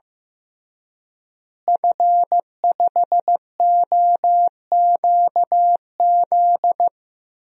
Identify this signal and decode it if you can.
Morse code, heard as a keyed tone, about 15 wpm, 700 Hz. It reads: F5OQZ
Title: DXCC-100 Call Signs - 5 Characters - Repeated 2x - 15wpm